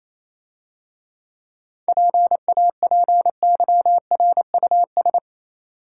Morse code, heard as a keyed tone, about 28 wpm, 700 Hz. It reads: PAPYRUS
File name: Easy Words - Speed-Racing with Hints - 22wpm